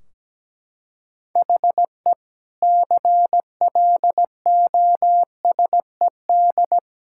Morse code, heard as a keyed tone, about 17 wpm, 700 Hz. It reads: HE CLOSED
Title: Sentences from Top 1600 Words 17wpm